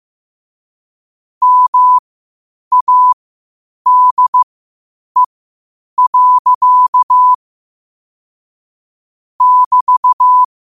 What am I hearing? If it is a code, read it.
Morse code, heard as a keyed tone, about 15 wpm, spaced as for 8 wpm, 1000 Hz. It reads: MADE. <BT>
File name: qotdfiles / 8wpm